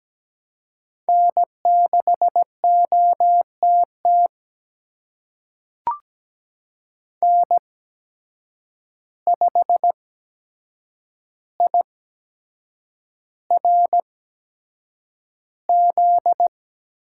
Morse code, heard as a keyed tone, about 17 wpm, spaced as for 7 wpm, 700 Hz. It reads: N6OTT N5IRZ
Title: USA Call Signs - 5 Characters - 8x Character Spacing - 17wpm